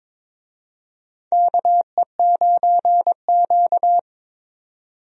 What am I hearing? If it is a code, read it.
Morse code, heard as a keyed tone, about 23 wpm, 700 Hz. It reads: KE9Q
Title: USA Call Signs - 4 Characters - 22wpm